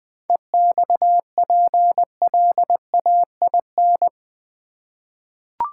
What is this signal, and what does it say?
Morse code, heard as a keyed tone, about 20 wpm, 700 Hz. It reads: EXPLAIN
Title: Sentences from Top 1100 Words - New Word Drill Alt v2 - 20wpm